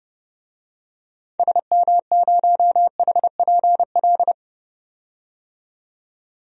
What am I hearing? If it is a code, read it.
Morse code, heard as a keyed tone, about 31 wpm, 700 Hz. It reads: SM0HPL
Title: DXCC-100 Call Signs - 6 Characters - 30wpm